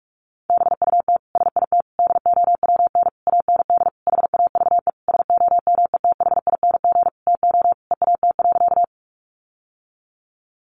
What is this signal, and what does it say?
Morse code, heard as a keyed tone, about 44 wpm, 700 Hz. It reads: BUT SIT DOWN AND HAVE SOMETHING TO EAT.
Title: Sentences from Top 600 Words 45wpm